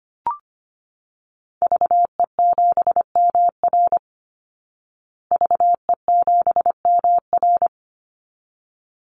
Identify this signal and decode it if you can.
Morse code, heard as a keyed tone, about 25 wpm, 700 Hz. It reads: VE7MR VE7MR
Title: Call Signs Repeated 3x - 25wpm